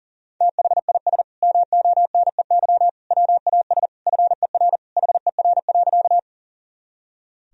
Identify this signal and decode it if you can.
Morse code, heard as a keyed tone, about 40 wpm, 700 Hz. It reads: THIS MONEY WAS FER HER.
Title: Sentences from Top 400 Words and Abbreviations 40wpm